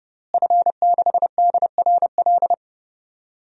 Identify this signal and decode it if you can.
Morse code, heard as a keyed tone, about 30 wpm, 700 Hz. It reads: F6DRL
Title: DXCC-100 Call Signs - 5 Characters - Repeated 3x - 30wpm